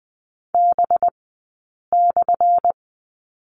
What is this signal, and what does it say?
Morse code, heard as a keyed tone, about 20 wpm, spaced as for 8 wpm, 700 Hz. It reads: B/
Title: 197 Lesson 32d - Letter k - Two Characters 20wpm